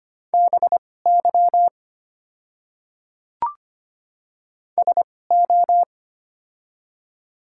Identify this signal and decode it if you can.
Morse code, heard as a keyed tone, about 25 wpm, spaced as for 18 wpm, 700 Hz. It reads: BY SO
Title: 2x Character Spacing - Top 100 Words - 25wpm